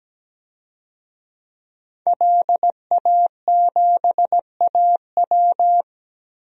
Morse code, heard as a keyed tone, about 17 wpm, 700 Hz. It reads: LA7AW
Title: DXCC-100 Call Signs - 5 Characters - Repeated 3x - 17wpm